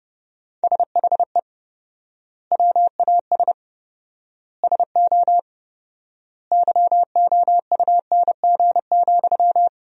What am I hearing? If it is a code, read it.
Morse code, heard as a keyed tone, about 32 wpm, 700 Hz. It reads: SHE WAS SO YOUNG,
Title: 4x Word-Spacing - Sentences from Top 400 Words - 30wpm